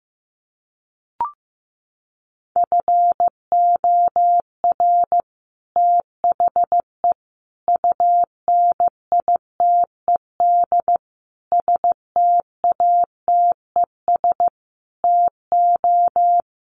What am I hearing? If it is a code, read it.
Morse code, heard as a keyed tone, about 15 wpm, 700 Hz. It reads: FOR THE UNITED STATES TO
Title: Sets of 5 Words Encore 15wpm